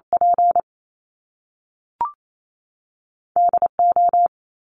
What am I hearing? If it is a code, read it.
Morse code, heard as a keyed tone, about 28 wpm, 700 Hz. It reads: P DO
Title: Top Two Letter Combinations in Words 28wpm